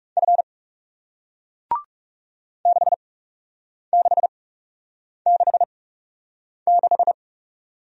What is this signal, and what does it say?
Morse code, heard as a keyed tone, about 40 wpm, spaced as for 8 wpm, 700 Hz. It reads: F 6666